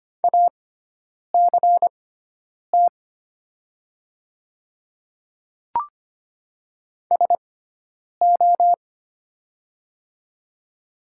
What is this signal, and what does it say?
Morse code, heard as a keyed tone, about 25 wpm, spaced as for 9 wpm, 690 Hz. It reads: ACT SO
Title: 6x Character Spacing - Top 500 Words - 25wpm